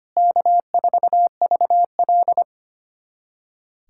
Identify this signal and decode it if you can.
Morse code, heard as a keyed tone, about 25 wpm, 700 Hz. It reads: K4VL
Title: USA Callsign DE Callsign - 4 Characters - 25wpm